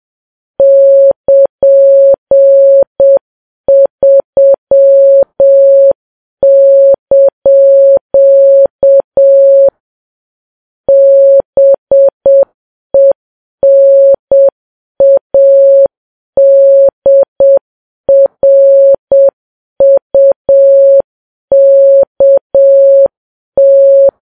Morse code, heard as a keyed tone, about 7 wpm, 550 Hz. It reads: <KN>3) BENADRUKT